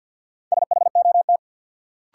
Morse code, heard as a keyed tone, about 52 wpm, 700 Hz. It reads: SHOT
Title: Sentences from Top 1100 Words - New Word Drill Alt - 50wpm